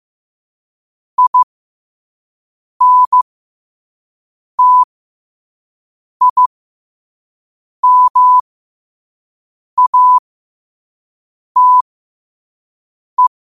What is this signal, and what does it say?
Morse code, heard as a keyed tone, about 15 wpm, spaced as for 5 wpm, 1000 Hz. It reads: INTIMATE
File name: qotdfiles / 5wpm